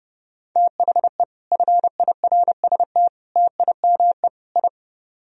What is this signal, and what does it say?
Morse code, heard as a keyed tone, about 32 wpm, 700 Hz. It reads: THE FIRST TIME I